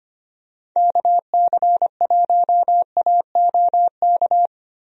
Morse code, heard as a keyed tone, about 25 wpm, 700 Hz. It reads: KC1AOK
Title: USA Call Signs - 6 Characters - Repeated 2x - 25wpm